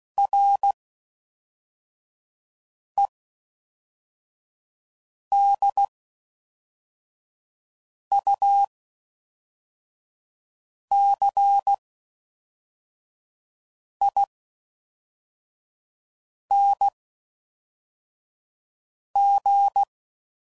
Morse code, heard as a keyed tone, about 16 wpm, spaced as for 4 wpm, 800 Hz. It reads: REDUCING